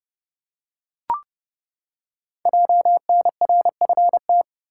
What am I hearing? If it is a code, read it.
Morse code, heard as a keyed tone, about 28 wpm, 700 Hz. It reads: JNRFT
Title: Code Groups - 5 Letter:Numbers - 30wpm